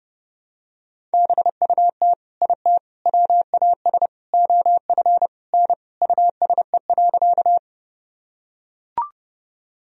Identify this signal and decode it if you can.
Morse code, heard as a keyed tone, about 30 wpm, 700 Hz. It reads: BUT IT WAS OF N USE.
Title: Sentences from Top 100 Words and Abbreviations 30wpm